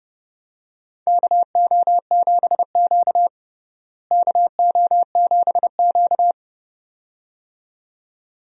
Morse code, heard as a keyed tone, about 29 wpm, 700 Hz. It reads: KO7Q KO7Q
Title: USA Call Signs - 4 Characters - Repeated 2x - 30wpm